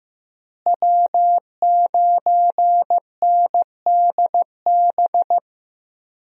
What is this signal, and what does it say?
Morse code, heard as a keyed tone, about 16 wpm, 700 Hz. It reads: W9NDB
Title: USA Call Signs - 5 Characters - Repeated 3x - 15wpm